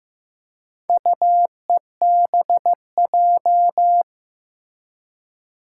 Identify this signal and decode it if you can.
Morse code, heard as a keyed tone, about 15 wpm, 700 Hz. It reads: UEBJ
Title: Four Challenging Letter-Number Combinations 15wpm